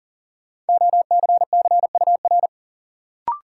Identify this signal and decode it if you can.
Morse code, heard as a keyed tone, about 40 wpm, 700 Hz. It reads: OCCUR
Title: Sentences from Top 900 Words - New Word Drill - 40wpm